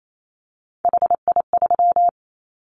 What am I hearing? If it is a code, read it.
Morse code, heard as a keyed tone, about 28 wpm, 700 Hz. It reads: HI3
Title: Callsign Prefixes for Top 100 DXCC - 28wpm